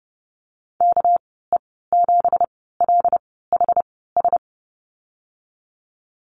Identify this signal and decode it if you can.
Morse code, heard as a keyed tone, about 30 wpm, spaced as for 17 wpm, 700 Hz. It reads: KE7LHS
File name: Lesson 37g - Letter x - Call Signs 30wpm